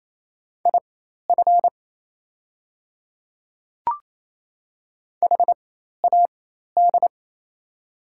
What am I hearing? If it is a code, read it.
Morse code, heard as a keyed tone, about 28 wpm, spaced as for 13 wpm, 690 Hz. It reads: IF HAD
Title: 4x Character Spacing - Top 100 Words - 28wpm